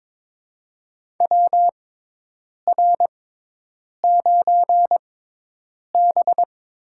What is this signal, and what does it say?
Morse code, heard as a keyed tone, about 22 wpm, spaced as for 8 wpm, 700 Hz. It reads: WR9B
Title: USA Call Signs - 4 Characters - 6x Character Spacing - 22wpm